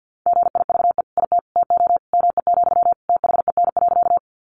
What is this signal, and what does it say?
Morse code, heard as a keyed tone, about 50 wpm, 700 Hz. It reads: GIVE IT TO ME, THEN.
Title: Sentences from Top 200 Words 50wpm